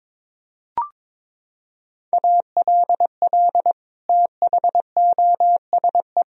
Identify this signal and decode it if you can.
Morse code, heard as a keyed tone, about 22 wpm, 700 Hz. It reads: ALL THOSE
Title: Sentences from Top 1000 Words - Review Part 1 - 22wpm